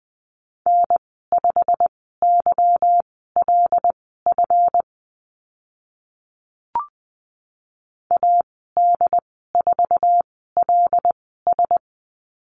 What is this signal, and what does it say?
Morse code, heard as a keyed tone, about 20 wpm, spaced as for 15 wpm, 700 Hz. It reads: N5YLF AD4LS
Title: 147 Lesson 24g - Letter y - Call Signs 20wpm